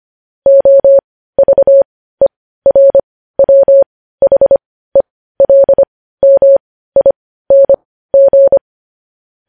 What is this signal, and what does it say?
Morse code, heard as a keyed tone, about 25 wpm, spaced as for 15 wpm, 550 Hz. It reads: OVERWHELMING
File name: podcast / News headlines-1x-25x15